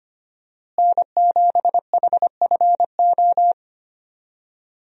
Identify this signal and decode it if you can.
Morse code, heard as a keyed tone, about 25 wpm, 700 Hz. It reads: N7HFO
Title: USA Call Signs - 5 Characters - Repeated 3x - 25wpm